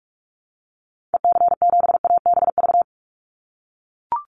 Morse code, heard as a keyed tone, about 44 wpm, 700 Hz. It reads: EC7ABV
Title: Call Signs - 45wpm